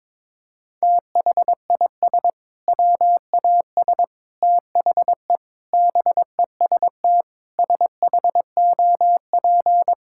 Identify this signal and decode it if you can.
Morse code, heard as a keyed tone, about 22 wpm, 700 Hz. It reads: THIS WAS THE BEST SHOP
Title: Sentences from Top 1600 Words 22wpm